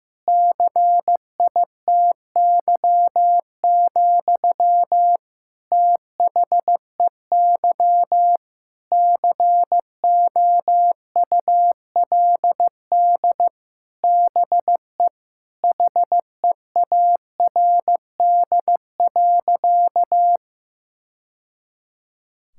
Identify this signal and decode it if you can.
Morse code, heard as a keyed tone, about 15 wpm, 700 Hz. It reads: CITY, THEY COULD BE HEARD.
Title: Sentences from Top 1000 Words - Review Part 5 - 15wpm